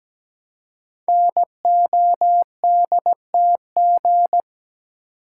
Morse code, heard as a keyed tone, about 17 wpm, 700 Hz. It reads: NODTG
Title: Code Groups - 5 Letters - 17wpm